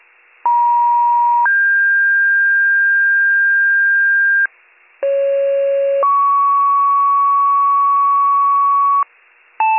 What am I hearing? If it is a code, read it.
Two-tone page, heard as a one-second tone, then a three-second tone, about 950 then 1600 Hz, about 550 then 1120 Hz, with noise. It is heard from 0.5 to 9.0 s.